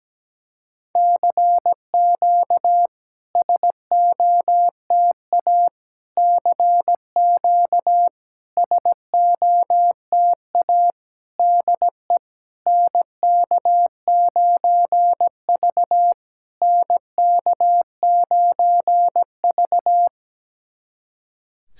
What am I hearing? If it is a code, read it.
Morse code, heard as a keyed tone, about 17 wpm, 700 Hz. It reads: CQ SOTA CQ SOTA DE NK9V NK9V
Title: SOTA QSO Practice 17wpm